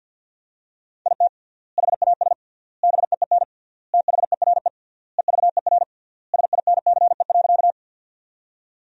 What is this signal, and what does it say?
Morse code, heard as a keyed tone, about 52 wpm, 700 Hz. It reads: IT HAS BEEN THERE EVER SINCE.
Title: 3x Word-Spacing - Sentences from Top 400 Words - 50wpm